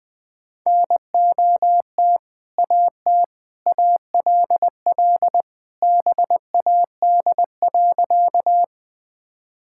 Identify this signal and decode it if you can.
Morse code, heard as a keyed tone, about 20 wpm, 700 Hz. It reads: NOT AT ALL BAD.
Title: Sentences with 2-Word Phrases - 20wpm